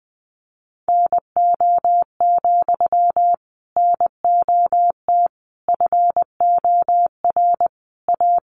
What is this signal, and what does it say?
Morse code, heard as a keyed tone, about 20 wpm, 700 Hz. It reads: NO, NOT FOR A
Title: Sentences from Top 1000 Words 20wpm